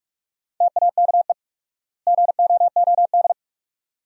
Morse code, heard as a keyed tone, about 42 wpm, 700 Hz. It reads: TAKE GOOD
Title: 4x Word-Spacing - Sentences from Top 500 Words - 45wpm